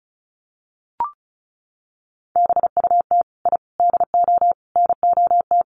Morse code, heard as a keyed tone, about 34 wpm, 700 Hz. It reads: BUT I DO NOT
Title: Sets of 4 Words 35wpm